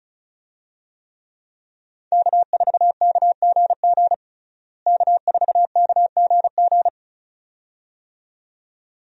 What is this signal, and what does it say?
Morse code, heard as a keyed tone, about 36 wpm, 700 Hz. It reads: K4KGG K4KGG
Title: USA Call Signs - 5 Characters - Repeated 2x - 35wpm